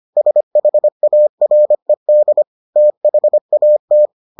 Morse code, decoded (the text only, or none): SHARED THAT